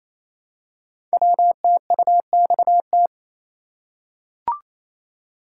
Morse code, heard as a keyed tone, about 27 wpm, 700 Hz. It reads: WTUXT
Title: Code Groups - 5 Letters - 28wpm